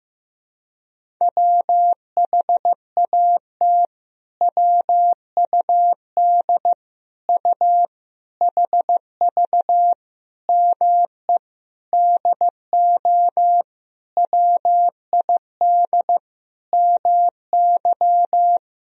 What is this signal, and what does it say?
Morse code, heard as a keyed tone, about 16 wpm, 700 Hz. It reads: WHAT WUD U HV ME DO WID MY
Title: Sentences from Top 300 Words and Abbreviations 15wpm